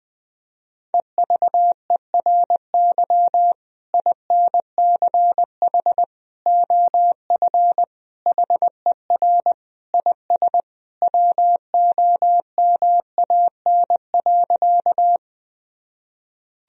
Morse code, heard as a keyed tone, about 20 wpm, 700 Hz. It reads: EVERY INCH OF HER IS WOMAN.
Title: Sentences from Top 1000 Words - Review Part 3 - 20wpm